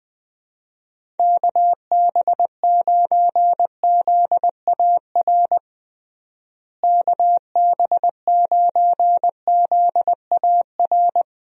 Morse code, heard as a keyed tone, about 20 wpm, 700 Hz. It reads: KB9ZAR KB9ZAR
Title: USA Call Signs - 6 Characters - Repeated 3x - 20wpm